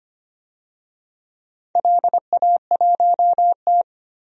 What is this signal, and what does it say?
Morse code, heard as a keyed tone, about 25 wpm, 700 Hz. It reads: LA1T
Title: DXCC-100 Call Signs - 4 Characters - 25wpm